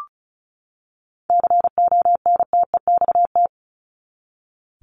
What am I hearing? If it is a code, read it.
Morse code, heard as a keyed tone, about 34 wpm, 700 Hz. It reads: CONTEXT
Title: Top 751-1000 Words 35wpm